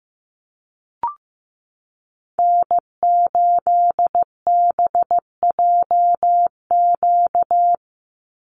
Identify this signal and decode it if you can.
Morse code, heard as a keyed tone, about 15 wpm, 700 Hz. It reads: N8BJQ
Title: Call Signs Repeated 3x - 15wpm